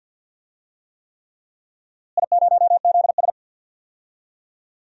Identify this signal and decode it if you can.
Morse code, heard as a keyed tone, about 50 wpm, 700 Hz. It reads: I0ZS
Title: DXCC-100 Call Signs - 4 Characters - Repeated 3x - 50wpm